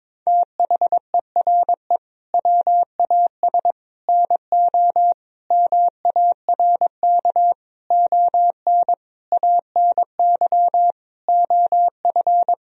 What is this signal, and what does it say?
Morse code, heard as a keyed tone, about 22 wpm, 700 Hz. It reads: THERE WAS NO MARK ON ANY OF
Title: Sentences from Top 1000 Words - Review Part 1 - 22wpm